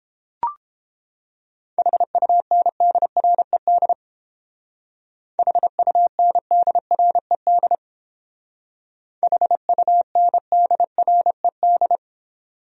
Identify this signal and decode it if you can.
Morse code, heard as a keyed tone, about 30 wpm, 700 Hz. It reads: HUNDRED HUNDRED HUNDRED